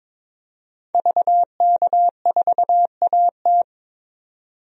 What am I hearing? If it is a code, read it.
Morse code, heard as a keyed tone, about 22 wpm, 700 Hz. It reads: VK4AT
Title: DXCC-100 Call Signs - 5 Characters - 22wpm